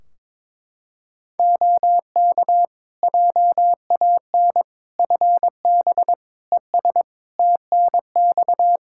Morse code, heard as a keyed tone, about 22 wpm, 700 Hz. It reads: OK JAN FB ES TNX
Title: K1USN SST QSO Practice 22wpm